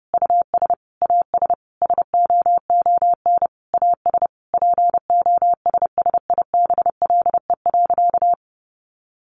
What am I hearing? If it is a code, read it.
Morse code, heard as a keyed tone, about 30 wpm, 700 Hz. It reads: US AS SOON AS POSSIBLE.
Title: Sentences from Top 500 Words 30wpm